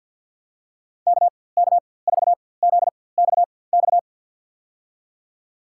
Morse code, heard as a keyed tone, about 53 wpm, spaced as for 24 wpm, 700 Hz. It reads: KK4ZXX
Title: USA Call Signs - 4x Character Spacing - 50wpm